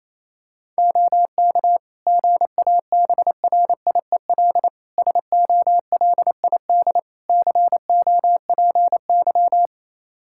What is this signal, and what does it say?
Morse code, heard as a keyed tone, about 28 wpm, 700 Hz. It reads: OK GABRIEL SOLID COPY
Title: Practice QSO - Exchange 3 - More Info - 28wpm